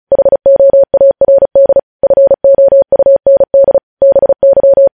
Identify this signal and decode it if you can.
Morse code, heard as a keyed tone, about 36 wpm, 550 Hz. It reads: HOARD FOUND BY
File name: podcast / News headlines-1x-35x35